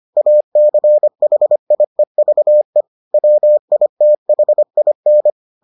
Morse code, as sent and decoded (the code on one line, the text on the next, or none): .- -.-. .... .. ...- . / .-- .. - .... .. -.
ACHIEVE WITHIN